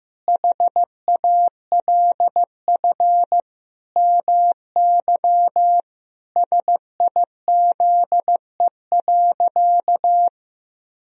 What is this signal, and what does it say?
Morse code, heard as a keyed tone, about 16 wpm, 700 Hz. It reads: HALF MY SIZE.
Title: Sentences from Top 1000 Words - Review Part 2 - 15wpm